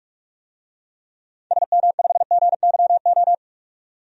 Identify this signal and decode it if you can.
Morse code, heard as a keyed tone, about 43 wpm, 700 Hz. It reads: SM5GYO
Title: DXCC-100 Call Signs - 6 Characters - 45wpm